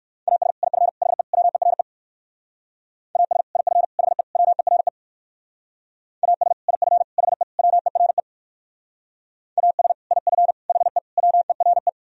Morse code, heard as a keyed tone, about 63 wpm, 700 Hz. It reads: AS IF HE WERE AS IF HE WERE AS IF HE WERE AS IF HE WERE